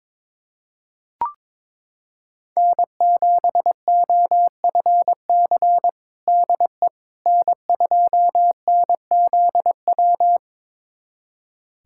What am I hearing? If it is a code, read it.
Morse code, heard as a keyed tone, about 21 wpm, 700 Hz. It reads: N7OFC DE N2NZW
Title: USA Callsign DE Callsign - 5 Characters - 22wpm